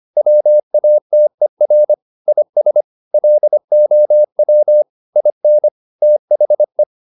Morse code, heard as a keyed tone, about 25 wpm, 600 Hz. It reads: WATER IS LOW IN THE
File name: the old man and the sea part 1 25wpm